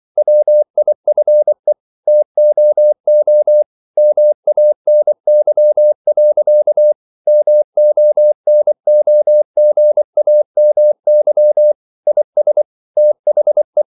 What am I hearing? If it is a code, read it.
Morse code, heard as a keyed tone, about 24 wpm, 600 Hz. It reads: WIFE TOO MANY. MONOGAMY IS THE
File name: CW / Practice / EnglishQuotations0324